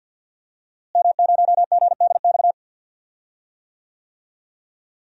Morse code, heard as a keyed tone, about 47 wpm, 700 Hz. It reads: M0GDX
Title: DXCC-100 Call Signs - 5 Characters - Repeated 2x - 50wpm